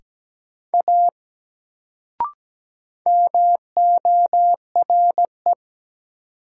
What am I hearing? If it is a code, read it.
Morse code, heard as a keyed tone, about 17 wpm, 700 Hz. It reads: A MORE
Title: Sentences from Top 100 Words - New Word Drill - 17wpm